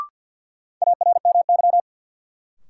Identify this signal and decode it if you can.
Morse code, heard as a keyed tone, about 52 wpm, 700 Hz. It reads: ARMY
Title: Sentences from Top 800 Words - New Word Drill - 50wpm